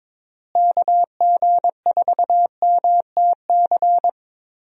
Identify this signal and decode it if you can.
Morse code, heard as a keyed tone, about 22 wpm, 700 Hz. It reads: KG4MTC
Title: USA Call Signs - 6 Characters - Repeated 3x - 22wpm